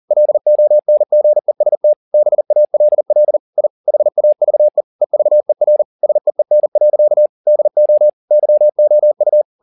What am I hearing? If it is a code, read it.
Morse code, heard as a keyed tone, about 40 wpm, 600 Hz. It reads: LONGEST BALL I HAVE EVER SEEN. DO YOU